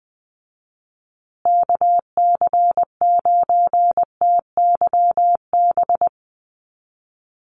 Morse code, heard as a keyed tone, about 19 wpm, 700 Hz. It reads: KC9TYB
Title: USA Call Signs Repeated 3x 20wpm